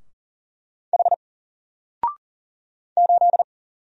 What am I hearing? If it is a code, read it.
Morse code, heard as a keyed tone, about 40 wpm, 700 Hz. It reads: H 8
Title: Single Letter-Number Alt 40wpm